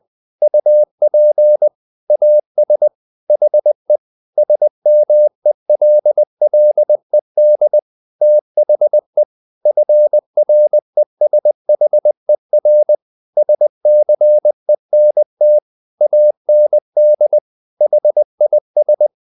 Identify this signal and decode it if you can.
Morse code, heard as a keyed tone, about 20 wpm, 600 Hz. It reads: UP AS HE SMELLED THE FRESHER SCENT AND HIS
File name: the old man and the sea part 2 20wpm